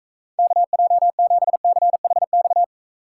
Morse code, heard as a keyed tone, about 44 wpm, 700 Hz. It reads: KJ7CHX